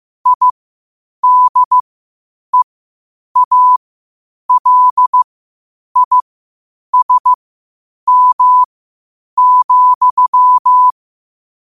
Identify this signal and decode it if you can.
Morse code, heard as a keyed tone, about 15 wpm, spaced as for 8 wpm, 1000 Hz. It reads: IDEALISM,
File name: qotdfiles / 8wpm